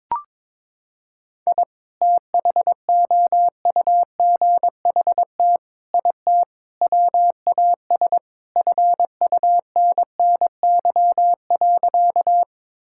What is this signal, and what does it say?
Morse code, heard as a keyed tone, about 22 wpm, 700 Hz. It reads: I THOUGHT IT WAS FUNNY.
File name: practice / 22wpm / Sentences with 4-Word Phrases - 22wpm